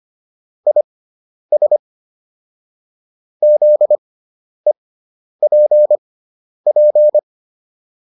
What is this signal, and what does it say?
Morse code, heard as a keyed tone, about 25 wpm, spaced as for 10 wpm, 600 Hz. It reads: IS ZEPP
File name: QSO401 10